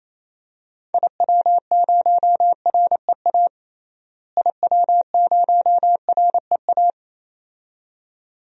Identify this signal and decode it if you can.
Morse code, heard as a keyed tone, about 27 wpm, 700 Hz. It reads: IW0REA IW0REA
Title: DXCC-100 Call Signs - 6 Characters - Repeated 2x - 28wpm